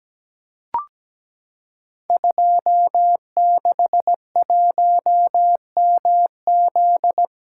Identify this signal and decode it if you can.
Morse code, heard as a keyed tone, about 17 wpm, 700 Hz. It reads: 261MZ
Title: Code Groups - 5 Letter:Numbers - 17wpm